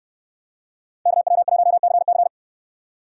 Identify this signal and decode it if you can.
Morse code, heard as a keyed tone, about 69 wpm, 700 Hz. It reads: KW1CC